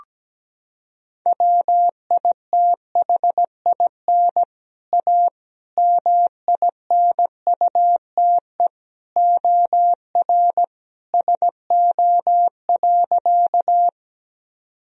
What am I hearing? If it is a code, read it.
Morse code, heard as a keyed tone, about 17 wpm, 700 Hz. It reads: WITHIN A MINUTE OR SO.
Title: Sentences from Top 1000 Words - Review Part 4 - 17wpm